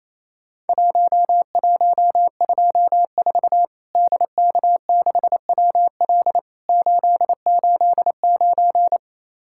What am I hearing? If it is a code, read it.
Morse code, heard as a keyed tone, about 28 wpm, 700 Hz. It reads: 1124 DK6WL 889
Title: Worked All Europe DX Contest - QTC Exchange - 28wpm